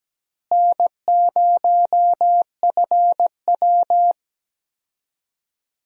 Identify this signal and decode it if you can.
Morse code, heard as a keyed tone, about 17 wpm, 700 Hz. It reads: N0FW
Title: USA Call Signs - 4 Characters - Repeated 2x - 17wpm